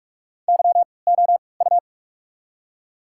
Morse code, heard as a keyed tone, about 47 wpm, spaced as for 26 wpm, 700 Hz. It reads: YOU
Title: 3x Character Spacing - Top 300 Words - 45wpm